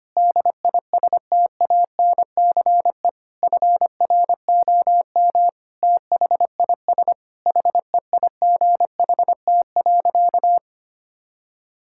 Morse code, heard as a keyed tone, about 25 wpm, 700 Hz. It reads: DISTANCE FROM THIS HEIGHT.
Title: Sentences from Top 1600 Words 25wpm